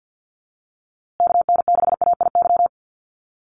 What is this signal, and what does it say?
Morse code, heard as a keyed tone, about 50 wpm, 700 Hz. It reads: KN6AIY
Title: USA Call Signs Repeated 2x 50wpm